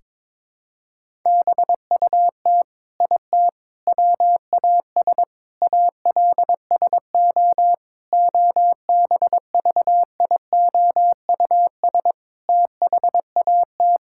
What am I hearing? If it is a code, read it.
Morse code, heard as a keyed tone, about 22 wpm, 700 Hz. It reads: BUT IT WAS ALSO OBVIOUS THAT
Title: Sentences from Top 1600 Words 22wpm